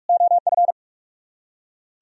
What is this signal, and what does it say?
Morse code, heard as a keyed tone, about 58 wpm, 700 Hz. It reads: OF